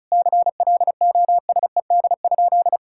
Morse code, heard as a keyed tone, about 36 wpm, 700 Hz. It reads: CLOSED?
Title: Sentences from Top 1600 Words 35wpm